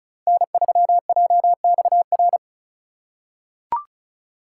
Morse code, heard as a keyed tone, about 34 wpm, 700 Hz. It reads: N3JXR
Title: USA Call Signs - 2x Character Spacing - 35wpm